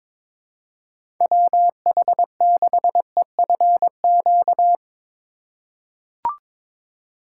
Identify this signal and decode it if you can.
Morse code, heard as a keyed tone, about 22 wpm, 700 Hz. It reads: WH6EFQ
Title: USA Call Signs - 6 Characters - 4x Character Spacing - 22wpm